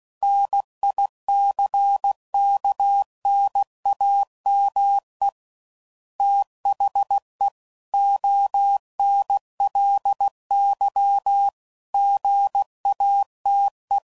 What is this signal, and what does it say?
Morse code, heard as a keyed tone, about 16 wpm, 800 Hz. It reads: NICKNAME THE ONLY GATE
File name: CW Code Groups @15WPM / Technical Subjects / AND Gate